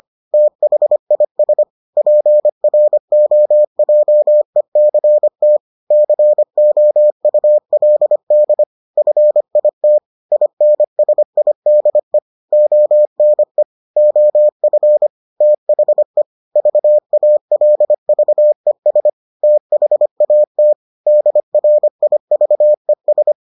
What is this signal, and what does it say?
Morse code, heard as a keyed tone, about 25 wpm, 600 Hz. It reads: THIS PROJECT COULD FIT INSIDE ONE OF THE VALVES THAT DRIVES